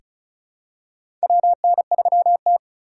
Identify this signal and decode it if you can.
Morse code, heard as a keyed tone, about 36 wpm, 700 Hz. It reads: WN3T
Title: USA Call Signs - 4 Characters - 3x Character Spacing - 35wpm